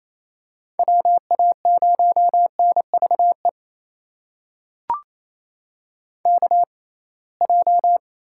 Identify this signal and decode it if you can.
Morse code, heard as a keyed tone, about 27 wpm, spaced as for 21 wpm, 700 Hz. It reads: WA0NVE KJ
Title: USA Call Signs - 6 Characters - 6x Character Spacing - 28wpm